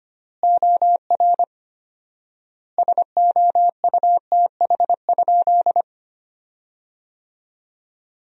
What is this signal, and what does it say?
Morse code, heard as a keyed tone, about 25 wpm, 700 Hz. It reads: OR SOUTH?
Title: 4x Word-Spacing - Sentences from Top 400 Words - 25wpm